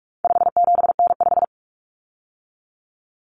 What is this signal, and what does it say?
Morse code, heard as a keyed tone, about 44 wpm, 700 Hz. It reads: 57N5
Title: Common Cut Numbers - 4-Digit 45wpm